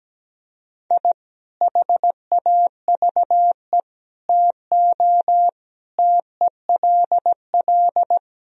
Morse code, heard as a keyed tone, about 17 wpm, 700 Hz. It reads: I HAVE TO TELL
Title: Speed-Racing - Sets of 5 Words 17wpm